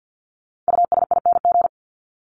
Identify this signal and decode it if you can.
Morse code, heard as a keyed tone, about 50 wpm, 700 Hz. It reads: USING
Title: ING Suffix - Top 100 Words with Base Words 50wpm